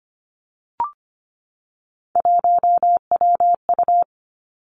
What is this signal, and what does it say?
Morse code, heard as a keyed tone, about 25 wpm, 700 Hz. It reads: 1WU
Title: Three Challenging Letter-Number Combinations 25wpm